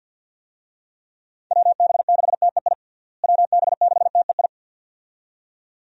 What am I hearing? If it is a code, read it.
Morse code, heard as a keyed tone, about 46 wpm, 700 Hz. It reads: WB6TEI WB6TEI
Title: USA Call Signs - 6 Characters - Repeated 2x - 50wpm